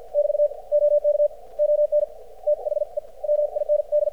Morse code, heard as a keyed tone, about 56 wpm, spaced as for 47 wpm, 590 Hz. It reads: <BT> OK ON THE MIND
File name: archived / CRSnet